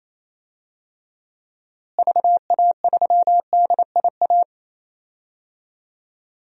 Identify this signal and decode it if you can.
Morse code, heard as a keyed tone, about 27 wpm, 700 Hz. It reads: VA3DIA
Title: DXCC-100 Call Signs - 6 Characters - Repeated 2x - 28wpm